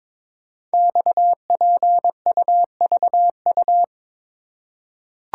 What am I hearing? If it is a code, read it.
Morse code, heard as a keyed tone, about 22 wpm, 700 Hz. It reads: XPUVU
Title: Code Groups - 5 Letters - 22wpm